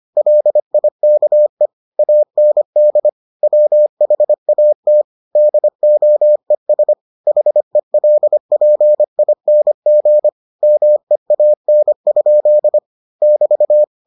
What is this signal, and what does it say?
Morse code, heard as a keyed tone, about 25 wpm, 600 Hz. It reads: LIKE AND WHAT DOES HELPING MEAN? <BT>